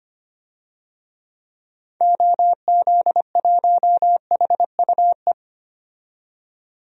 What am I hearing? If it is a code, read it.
Morse code, heard as a keyed tone, about 25 wpm, 700 Hz. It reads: OZ1HUE
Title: DXCC-100 Call Signs - 6 Characters - 25wpm